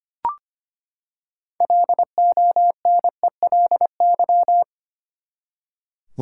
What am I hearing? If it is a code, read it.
Morse code, heard as a keyed tone, about 25 wpm, 700 Hz. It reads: LONELY
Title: Six Letter Words - 25wpm